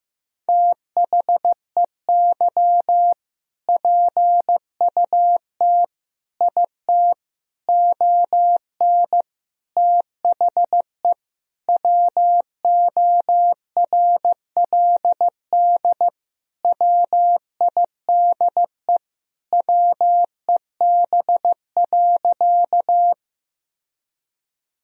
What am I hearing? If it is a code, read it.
Morse code, heard as a keyed tone, about 15 wpm, 700 Hz. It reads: THEY PUT IT ON THE WORLD WIDE WEB.
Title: Sentences from Top 1000 Words - Review Part 4 - 15wpm